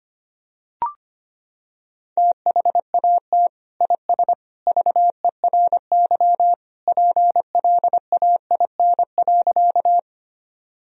Morse code, heard as a keyed tone, about 25 wpm, 700 Hz. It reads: THAT IS VERY PLAIN.
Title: Sentences from Top 1000 Words - Review Part 2 - 25wpm